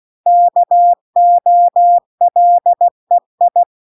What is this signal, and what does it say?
Morse code, heard as a keyed tone, about 16 wpm, 700 Hz. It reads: KOLEI